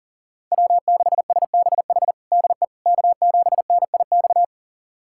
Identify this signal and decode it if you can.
Morse code, heard as a keyed tone, about 40 wpm, 700 Hz. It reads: W6SBH DE K7NIX
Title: USA Callsign DE Callsign 40wpm